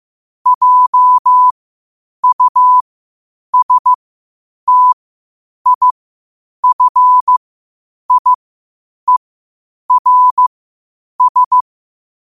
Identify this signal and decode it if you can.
Morse code, heard as a keyed tone, about 15 wpm, spaced as for 8 wpm, 1000 Hz. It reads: JUSTIFIERS
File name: qotdfiles / 8wpm